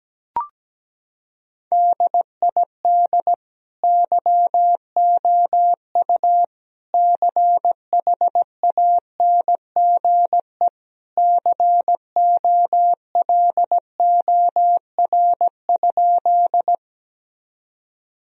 Morse code, heard as a keyed tone, about 17 wpm, 700 Hz. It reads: DID YOU CHANGE COLOR?
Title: Sentences from Top 1000 Words - Review Part 2 - 17wpm